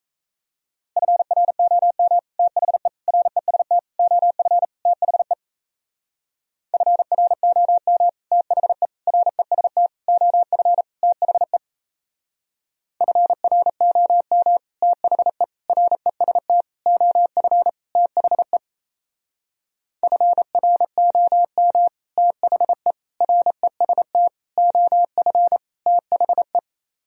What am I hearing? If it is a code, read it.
Morse code, heard as a keyed tone, about 36 wpm, 700 Hz. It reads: FROM THE REST OF THE FROM THE REST OF THE FROM THE REST OF THE FROM THE REST OF THE